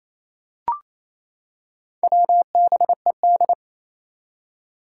Morse code, heard as a keyed tone, about 29 wpm, 700 Hz. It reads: WBED
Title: Four Challenging Letter-Number Combinations 28wpm